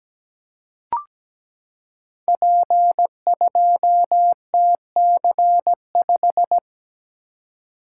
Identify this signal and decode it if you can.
Morse code, heard as a keyed tone, about 17 wpm, 700 Hz. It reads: P2TC5
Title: Code Groups - 5 Letter:Numbers - 17wpm